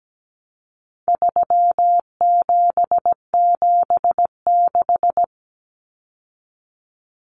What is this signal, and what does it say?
Morse code, heard as a keyed tone, about 17 wpm, 700 Hz. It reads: 3776
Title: Numbers - 4-Digit 17wpm